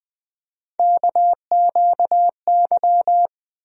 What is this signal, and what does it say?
Morse code, heard as a keyed tone, about 20 wpm, 700 Hz. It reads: KQY
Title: USA Call Sign Postfixes - 3 Letters 20wpm